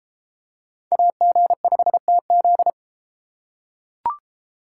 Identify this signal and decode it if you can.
Morse code, heard as a keyed tone, about 30 wpm, 700 Hz. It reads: AG5TZ